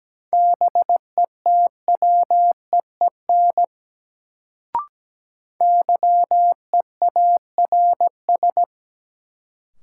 Morse code, heard as a keyed tone, about 17 wpm, 700 Hz. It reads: BETWEEN YEARS
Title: Sets of 2 Words - New Word Drill - 17wpm